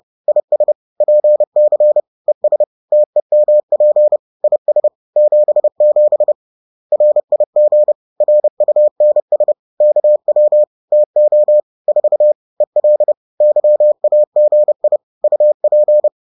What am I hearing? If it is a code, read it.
Morse code, heard as a keyed tone, about 30 wpm, 600 Hz. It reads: IS PC ES TEMP IS 77 RIG RUNS KW TO 4 EL YAGI UP
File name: 30wpm